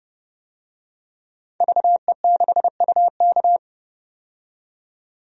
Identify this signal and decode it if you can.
Morse code, heard as a keyed tone, about 30 wpm, 700 Hz. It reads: VE6UK
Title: DXCC-100 Call Signs - 5 Characters - 30wpm